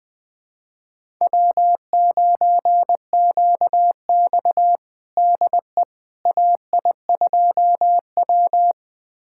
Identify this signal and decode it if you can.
Morse code, heard as a keyed tone, about 20 wpm, 700 Hz. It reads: W9QX DE AI2W
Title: USA Callsign DE Callsign - 4 Characters - 20wpm